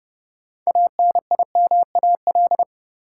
Morse code, heard as a keyed tone, about 30 wpm, 700 Hz. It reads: ANIMAL
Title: Sentences from Top 1000 Words - Review Part 1 - New Word Drill - 30wpm